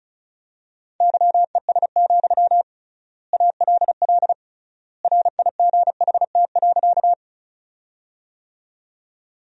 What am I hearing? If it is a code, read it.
Morse code, heard as a keyed tone, about 34 wpm, 700 Hz. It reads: YES, ALL RIGHT.
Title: 3x Word-Spacing - Sentences from Top 500 Words - 35wpm